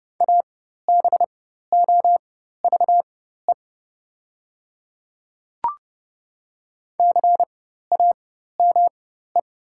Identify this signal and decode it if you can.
Morse code, heard as a keyed tone, about 29 wpm, spaced as for 14 wpm, 690 Hz. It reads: ABOVE CAME
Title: 4x Character Spacing - Top 500 Words - 30wpm